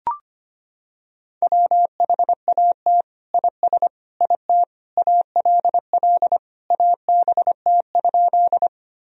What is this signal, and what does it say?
Morse code, heard as a keyed tone, about 25 wpm, 700 Hz. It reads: WHAT IS IT ALL ABT?
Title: Sentences from Top 100 Words and Abbreviations 25wpm